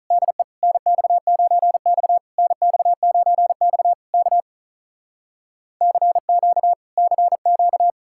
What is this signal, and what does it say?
Morse code, heard as a keyed tone, about 48 wpm, 700 Hz. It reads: DE NX9X NX9X K CQ CQ